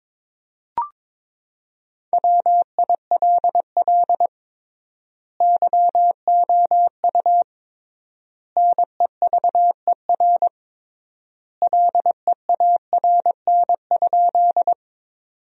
Morse code, heard as a keyed tone, about 22 wpm, 700 Hz. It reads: WILL YOU NEVER LEARN?
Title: 3x Word-Spacing - Sentences from Top 300 Words - 22wpm